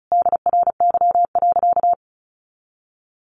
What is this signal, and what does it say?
Morse code, heard as a keyed tone, about 34 wpm, 700 Hz. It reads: DRY.
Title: Sentences from Top 500 Words 35wpm